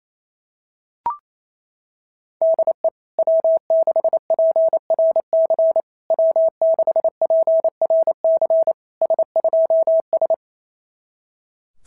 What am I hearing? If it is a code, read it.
Morse code, heard as a keyed tone, about 28 wpm, 660 Hz. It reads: DE W6PRC W6PRC S2S
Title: SOTA QSO Practice 28wpm